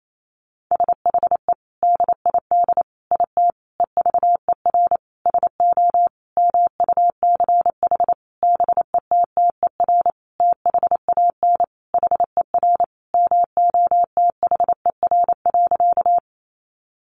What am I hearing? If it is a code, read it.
Morse code, heard as a keyed tone, about 28 wpm, 700 Hz. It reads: SHE DID IT EVER SO MUCH BETTER THAN HER MOTHER.
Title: Sentences from Top 400 Words 28wpm